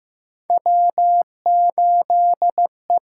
Morse code, heard as a keyed tone, about 15 wpm, 700 Hz. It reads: W8E
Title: USA Call Signs - 6x Character Spacing - 15wpm